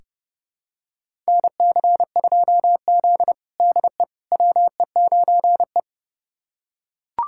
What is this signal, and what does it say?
Morse code, heard as a keyed tone, about 30 wpm, 700 Hz. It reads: NC2Z DE WE9E
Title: USA Callsign DE Callsign - 4 Characters - 30wpm